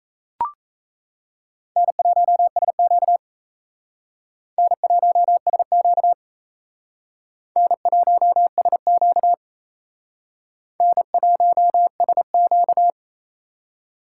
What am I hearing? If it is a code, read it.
Morse code, heard as a keyed tone, about 35 wpm, 700 Hz. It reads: N1SQ N1SQ N1SQ N1SQ
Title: Speed Racing - USA 1x2 Call Signs 28wpm